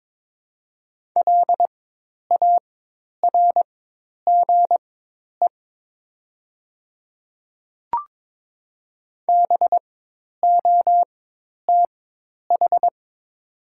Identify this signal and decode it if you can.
Morse code, heard as a keyed tone, about 22 wpm, spaced as for 10 wpm, 700 Hz. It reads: LARGE BOTH
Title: 4x Character Spacing - Top 500 Words - 22wpm